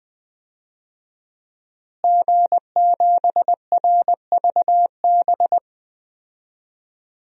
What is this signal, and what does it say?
Morse code, heard as a keyed tone, about 20 wpm, 700 Hz. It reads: G7RVB
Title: DXCC-100 Call Signs - 5 Characters - Repeated 2x - 20wpm